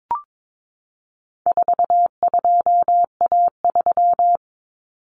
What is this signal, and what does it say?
Morse code, heard as a keyed tone, about 22 wpm, 700 Hz. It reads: 42A3
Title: Common Cut Numbers - 4-Digit 22wpm